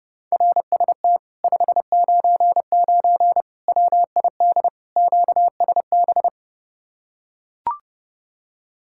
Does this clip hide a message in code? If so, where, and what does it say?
Yes: Morse code from 0.3 to 6.3 s, heard as a keyed tone, about 31 wpm, 700 Hz. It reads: RST 599 WID QSB